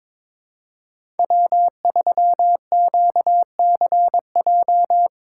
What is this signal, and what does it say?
Morse code, heard as a keyed tone, about 22 wpm, 700 Hz. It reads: W3QCJ
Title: USA Call Signs - 2x Character Spacing - 22wpm